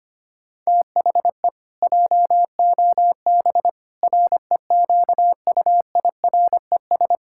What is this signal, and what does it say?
Morse code, heard as a keyed tone, about 25 wpm, 700 Hz. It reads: THE JOB REQUIRES
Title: Pangram Challenge v2 - 25wpm